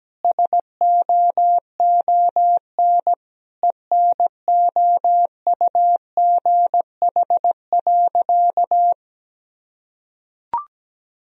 Morse code, heard as a keyed tone, about 17 wpm, 700 Hz. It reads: SOON ENOUGH.
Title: Sentences from Top 1000 Words - Review Part 2 - 17wpm